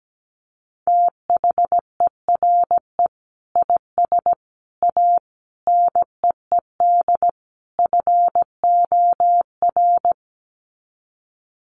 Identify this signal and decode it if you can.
Morse code, heard as a keyed tone, about 17 wpm, 700 Hz. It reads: THERE IS A NEED FOR
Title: Sets of 5 Words Encore2 17wpm